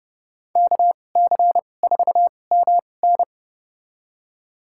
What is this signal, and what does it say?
Morse code, heard as a keyed tone, about 30 wpm, spaced as for 22 wpm, 700 Hz. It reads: KC4MN
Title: USA Call Signs - 5 Characters - 2x Character Spacing - 30wpm